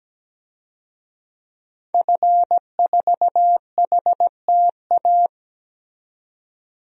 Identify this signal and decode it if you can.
Morse code, heard as a keyed tone, about 17 wpm, 700 Hz. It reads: F4HTA
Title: DXCC-100 Call Signs - 5 Characters - 17wpm